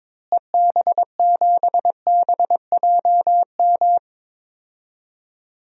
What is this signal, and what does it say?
Morse code, heard as a keyed tone, about 22 wpm, 700 Hz. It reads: EB7BJM
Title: DXCC-100 Call Signs - 6 Characters - Repeated 3x - 22wpm